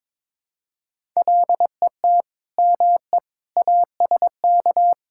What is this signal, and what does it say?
Morse code, heard as a keyed tone, about 22 wpm, 700 Hz. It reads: LET ME ASK
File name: practice / 22wpm / Build A Phrase - 5-Word Phrases - 22wpm